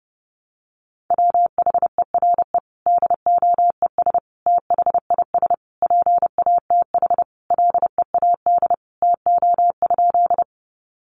Morse code, heard as a keyed tone, about 28 wpm, 700 Hz. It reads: WHERE DOES THIS PATH LEAD TO?
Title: Sentences from Top 1400 Words 30wpm